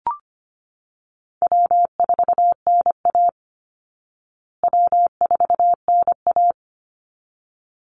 Morse code, heard as a keyed tone, about 25 wpm, 700 Hz. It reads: W4NA W4NA
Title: Call Signs Repeated 3x - 25wpm